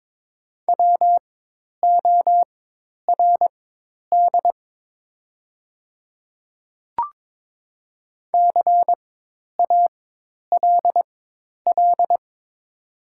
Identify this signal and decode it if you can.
Morse code, heard as a keyed tone, about 22 wpm, spaced as for 10 wpm, 700 Hz. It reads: WORD CALL
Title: 4x Character Spacing - Top 100 Words - 22wpm